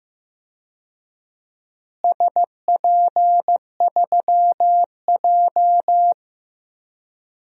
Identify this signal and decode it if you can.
Morse code, heard as a keyed tone, about 15 wpm, 700 Hz. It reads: SP3J
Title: DXCC-100 Call Signs - 4 Characters - Repeated 3x - 15wpm